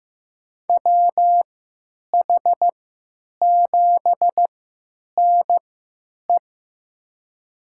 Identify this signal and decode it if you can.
Morse code, heard as a keyed tone, about 15 wpm, spaced as for 9 wpm, 700 Hz. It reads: WH7NE
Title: USA Call Signs - 5 Characters - 3x Character Spacing - 15wpm